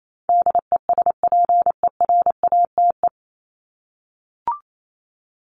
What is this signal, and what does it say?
Morse code, heard as a keyed tone, about 28 wpm, 700 Hz. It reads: DESPERATE
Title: Top 1751-2000 Words 28wpm